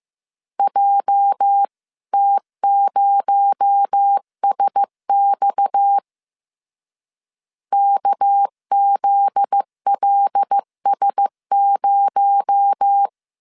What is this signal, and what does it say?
Morse code, heard as a keyed tone, about 15 wpm, 800 Hz. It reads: JT0SX KZLS0
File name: CW / DET / 1 9 1500C S C 14wpm 3245